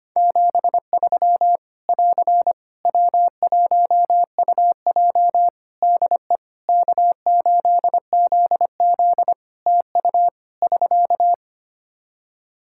Morse code, heard as a keyed tone, about 25 wpm, 700 Hz. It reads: 73 <AR> W1UJ DE K8ZZ TU <SK>
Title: Practice QSO - Exchange 4 - Ending - 25wpm